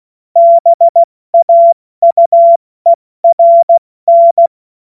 Morse code, heard as a keyed tone, about 16 wpm, 680 Hz. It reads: BAUERN
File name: Morsekurs / Witze / Witz 07. Februar 2023 70 F80